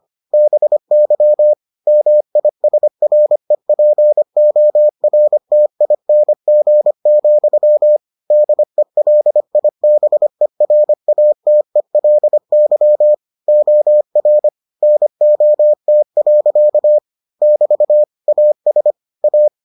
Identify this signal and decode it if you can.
Morse code, heard as a keyed tone, about 25 wpm, 600 Hz. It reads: BY MISREPORTING, DELIBERATELY OR NOT. <BT> AS A